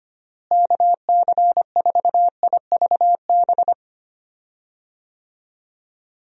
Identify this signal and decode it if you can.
Morse code, heard as a keyed tone, about 25 wpm, 700 Hz. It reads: KC4IVB
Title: USA Call Signs - 6 Characters - Repeated 3x - 25wpm